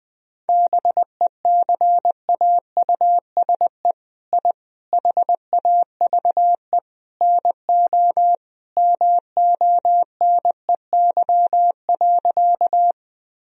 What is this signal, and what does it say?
Morse code, heard as a keyed tone, about 20 wpm, 700 Hz. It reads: BECAUSE I HAVE NO MONEY.
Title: Sentences from Top 1000 Words - Review Part 3 - 20wpm